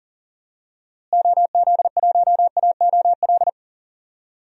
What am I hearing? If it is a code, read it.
Morse code, heard as a keyed tone, about 37 wpm, 700 Hz. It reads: OZ1AOL
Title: DXCC-100 Call Signs - 6 Characters - Repeated 2x - 40wpm